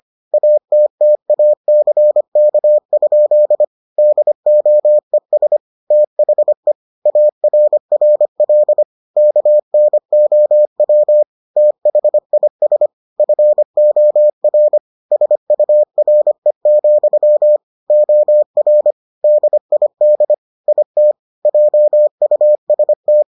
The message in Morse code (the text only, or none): ATTACK? DOES THE ARRL KNOW THIS FOR SURE, OR DID IT JUST